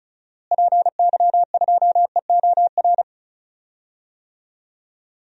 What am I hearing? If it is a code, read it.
Morse code, heard as a keyed tone, about 34 wpm, 700 Hz. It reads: PY2EOR
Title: DXCC-100 Call Signs - 6 Characters - Repeated 2x - 35wpm